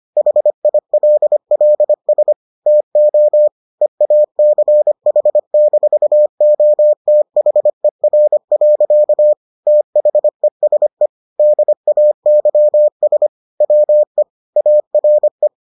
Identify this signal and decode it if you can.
Morse code, heard as a keyed tone, about 25 wpm, 600 Hz. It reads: HILLS TO EACH-OTHER. THESE DAYS WE ARE